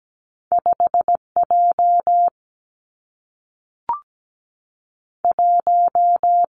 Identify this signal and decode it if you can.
Morse code, heard as a keyed tone, about 17 wpm, 700 Hz. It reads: 5 J 1
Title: Two Challenging Letter-Number Combinations 17wpm